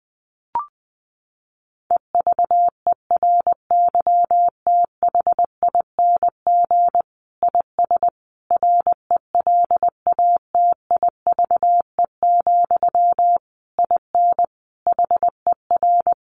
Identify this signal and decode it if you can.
Morse code, heard as a keyed tone, about 20 wpm, 700 Hz. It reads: EVERYTHING IS RELATIVE, IN HER
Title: Sentences from Top 1400 Words 20wpm